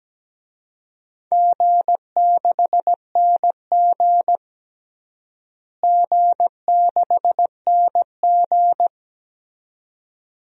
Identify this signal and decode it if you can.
Morse code, heard as a keyed tone, about 17 wpm, 700 Hz. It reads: G6NG G6NG
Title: DXCC-100 Call Signs - 4 Characters - Repeated 2x - 17wpm